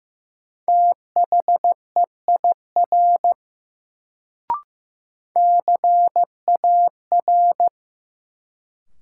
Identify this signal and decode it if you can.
Morse code, heard as a keyed tone, about 16 wpm, 700 Hz. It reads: THEIR CAR
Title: Sentences from Top 300 Words - New Word Drill - 15wpm